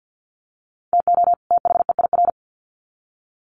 Morse code, heard as a keyed tone, about 50 wpm, 700 Hz. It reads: TO THEIR
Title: Sets of 2 Words 50wpm